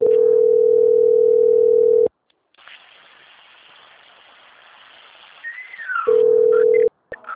Ringback tone (North America), 440 + 480 Hz.
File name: out- - - - - .22761